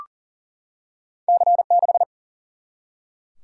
Morse code, heard as a keyed tone, about 40 wpm, 700 Hz. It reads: C6
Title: Single-Letter Single-Number 40wpm